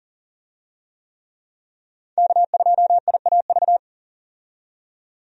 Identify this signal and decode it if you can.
Morse code, heard as a keyed tone, about 40 wpm, 700 Hz. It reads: K2IAV
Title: USA Call Signs - 5 Characters - 40wpm